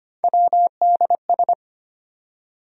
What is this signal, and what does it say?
Morse code, heard as a keyed tone, about 34 wpm, spaced as for 30 wpm, 700 Hz. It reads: WDS
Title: Speed-Racing Abbreviations - 25wpm